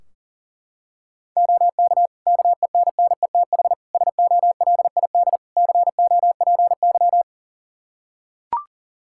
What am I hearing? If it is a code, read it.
Morse code, heard as a keyed tone, about 40 wpm, 700 Hz. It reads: OK KENNETH SOLID COPY
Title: Practice QSO - Exchange 3 - More Info - 40wpm